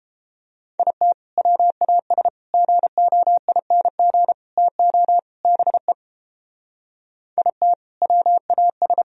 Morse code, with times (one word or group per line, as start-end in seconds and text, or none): 0.8-1.1 IT
1.4-2.3 WAS
2.5-4.3 GOING
4.6-5.2 TO
5.4-5.9 BE
7.4-7.7 IT
8.0-9.0 WAS